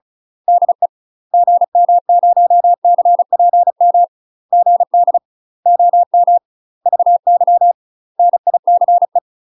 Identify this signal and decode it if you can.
Morse code, heard as a keyed tone, about 34 wpm, 700 Hz. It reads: DE GM0CPM GD OM VY NICE
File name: UK MASTER QSOs TRAINING/35WPM